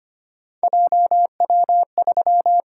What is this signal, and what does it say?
Morse code, heard as a keyed tone, about 25 wpm, 700 Hz. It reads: JW3
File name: Three Letter-Number Combinations - 25wpm